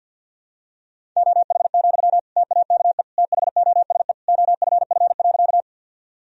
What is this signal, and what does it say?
Morse code, heard as a keyed tone, about 47 wpm, 700 Hz. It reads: OH, TAKE THOSE OFF.
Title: Sentences from Top 1000 Words - Review Part 1 - 50wpm